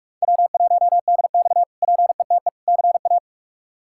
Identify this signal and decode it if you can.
Morse code, heard as a keyed tone, about 41 wpm, 700 Hz. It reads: W1DX PETE CA
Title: K1USN SST QSO Practice - 45wpm